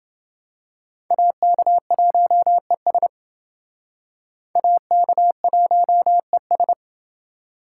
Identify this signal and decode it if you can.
Morse code, heard as a keyed tone, about 28 wpm, 700 Hz. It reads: AK1ES AK1ES